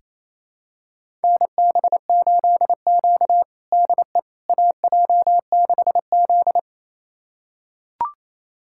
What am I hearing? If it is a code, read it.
Morse code, heard as a keyed tone, about 28 wpm, 700 Hz. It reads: NB8Q DE AJ6Z
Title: USA Callsign DE Callsign - 4 Characters - 28wpm